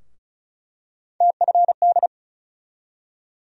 Morse code, heard as a keyed tone, about 36 wpm, 700 Hz. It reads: TFD
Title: USA Call Sign Postfixes - 3 Letters 35wpm